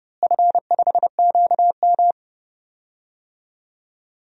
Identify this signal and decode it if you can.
Morse code, heard as a keyed tone, about 30 wpm, 700 Hz. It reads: F5QM
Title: DXCC-100 Call Signs - 4 Characters - Repeated 2x - 30wpm